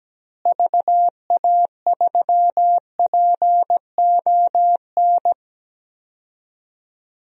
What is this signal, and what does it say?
Morse code, heard as a keyed tone, about 17 wpm, 700 Hz. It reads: VA3PON
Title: DXCC-100 Call Signs - 6 Characters - Repeated 3x - 17wpm